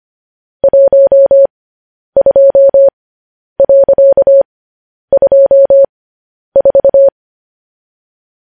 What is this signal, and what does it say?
Morse code, heard as a keyed tone, about 25 wpm, spaced as for 10 wpm, 550 Hz. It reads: 12.24
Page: news-headlines-in-morse-code-at-10-wpm